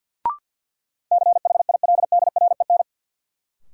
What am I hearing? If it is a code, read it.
Morse code, heard as a keyed tone, about 53 wpm, 700 Hz. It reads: CHILDREN
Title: Sentences from Top 300 Words - New Word Drill - 50wpm